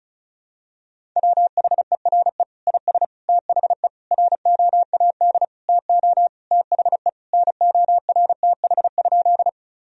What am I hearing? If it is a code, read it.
Morse code, heard as a keyed tone, about 36 wpm, 700 Hz. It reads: WHERE IS THE ROAD TO THE NORTH?
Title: Sentences from Top 1000 Words - Review Part 2 - 35wpm